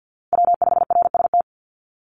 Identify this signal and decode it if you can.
Morse code, heard as a keyed tone, about 50 wpm, 700 Hz. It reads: W5RST
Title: NA SOTA Chaser Callsigns 50wpm